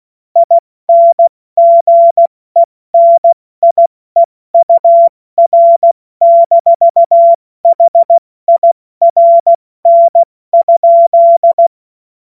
Morse code, heard as a keyed tone, about 16 wpm, 680 Hz. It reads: INGENIEUR-HIRN?
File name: Morsekurs / Witze / Witz 07. Juni 2022 70 F80